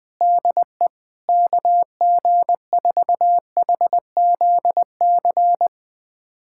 Morse code, heard as a keyed tone, about 21 wpm, 700 Hz. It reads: DE KG4HZC
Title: USA Callsign DE Callsign - 6 Characters - 20wpm